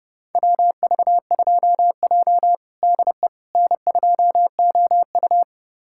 Morse code, heard as a keyed tone, about 31 wpm, 700 Hz. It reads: WV2J DE N2OU
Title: USA Callsign DE Callsign - 4 Characters - 30wpm